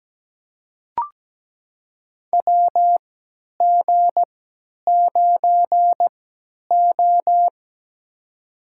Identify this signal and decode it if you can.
Morse code, heard as a keyed tone, about 17 wpm, spaced as for 10 wpm, 700 Hz. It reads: WG9O
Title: USA Call Signs - 4 Characters - 3x Character Spacing - 17wpm